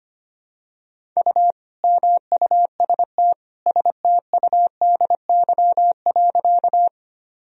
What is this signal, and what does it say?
Morse code, heard as a keyed tone, about 25 wpm, 700 Hz. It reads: U MUST STUDY.
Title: Sentences from Top 300 Words and Abbreviations 25wpm